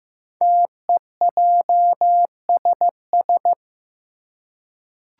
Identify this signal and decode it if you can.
Morse code, heard as a keyed tone, about 15 wpm, 700 Hz. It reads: TEJSS
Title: Code Groups - 5 Letters - 15wpm